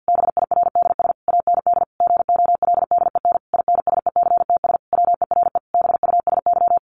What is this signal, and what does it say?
Morse code, heard as a keyed tone, about 50 wpm, 700 Hz. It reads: BIRDS AND GOLDEN INSECTS WERE BUSY
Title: A Sound Of Thunder 50wpm